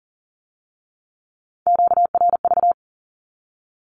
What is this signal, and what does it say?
Morse code, heard as a keyed tone, about 40 wpm, 700 Hz. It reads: QRV
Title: Q-Codes with Spoken Meaning - 40wpm